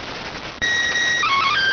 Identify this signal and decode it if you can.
SSTV (slow-scan television), PD 290.